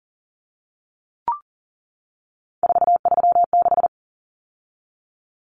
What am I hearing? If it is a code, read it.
Morse code, heard as a keyed tone, about 40 wpm, 700 Hz. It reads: 436
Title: Common Cut Numbers - 3-Digit 40wpm